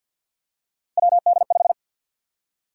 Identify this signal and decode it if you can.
Morse code, heard as a keyed tone, about 48 wpm, 700 Hz. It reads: WD5
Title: USA Call Sign Prefixes - Repeated 2x - 50wpm